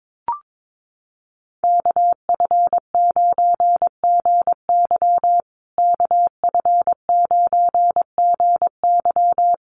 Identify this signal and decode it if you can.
Morse code, heard as a keyed tone, about 22 wpm, 700 Hz. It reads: KF9GY KF9GY
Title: USA Call Signs Repeated 2x 22wpm